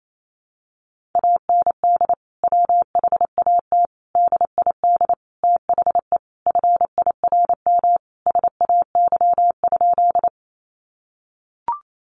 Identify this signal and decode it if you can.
Morse code, heard as a keyed tone, about 28 wpm, 700 Hz. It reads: AND WHAT DID THE FIRM SAY?
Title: Sentences from Top 1400 Words 28wpm